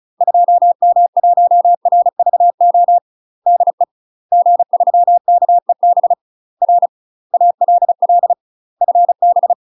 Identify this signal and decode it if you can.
Morse code, heard as a keyed tone, about 36 wpm, 700 Hz. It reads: 2M1RVO DE G3KEB R ALL FB
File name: UK MASTER QSOs TRAINING / 35WPM / UK MASTER QSOs 0009